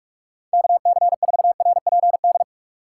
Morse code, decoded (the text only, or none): KC4RPD